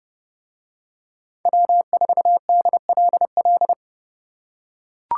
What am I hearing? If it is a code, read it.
Morse code, heard as a keyed tone, about 30 wpm, 700 Hz. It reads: W4DLL